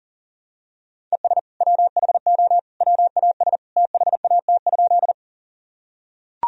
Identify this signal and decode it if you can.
Morse code, heard as a keyed tone, about 40 wpm, 700 Hz. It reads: ES WHO WAS THAT?
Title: Sentences from Top 100 Words and Abbreviations 40wpm